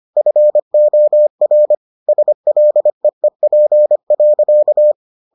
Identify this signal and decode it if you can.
Morse code, heard as a keyed tone, about 25 wpm, 600 Hz. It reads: FOR SLEEP.